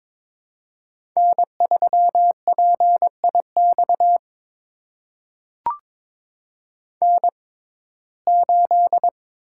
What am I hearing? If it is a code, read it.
Morse code, heard as a keyed tone, about 22 wpm, spaced as for 16 wpm, 700 Hz. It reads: N3PIX N8
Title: USA Call Signs - 5 Characters - 6x Character Spacing - 22wpm